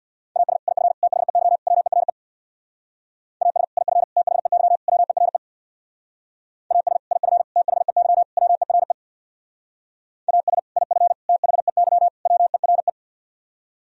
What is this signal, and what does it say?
Morse code, heard as a keyed tone, about 64 wpm, 700 Hz. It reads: AS IF THEY WERE AS IF THEY WERE AS IF THEY WERE AS IF THEY WERE